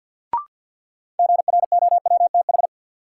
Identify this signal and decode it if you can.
Morse code, heard as a keyed tone, about 48 wpm, 700 Hz. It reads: GROWTH
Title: Sentences from Top 1000 Words - Review Part 5 - New Word Drill - 50wpm